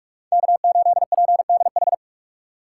Morse code, heard as a keyed tone, about 47 wpm, 690 Hz. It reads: K8PDH
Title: USA Call Signs - 5 Characters - 8x Character Spacing - 45wpm